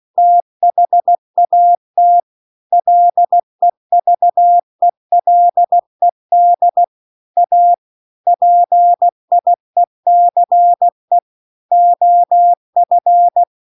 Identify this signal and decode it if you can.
Morse code, heard as a keyed tone, about 16 wpm, 700 Hz. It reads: THAT LEVELED A PIECE OF